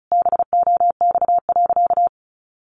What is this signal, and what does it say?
Morse code, heard as a keyed tone, about 34 wpm, 700 Hz. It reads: BOX.
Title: Sentences from Top 700 Words 35wpm